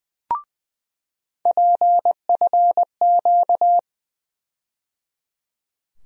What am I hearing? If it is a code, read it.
Morse code, heard as a keyed tone, about 20 wpm, 700 Hz. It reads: PFQ
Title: USA Call Sign Postfixes - 3 Letters 20wpm